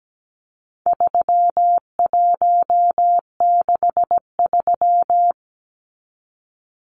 Morse code, heard as a keyed tone, about 17 wpm, 700 Hz. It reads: 3163
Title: Numbers - 4-Digit 17wpm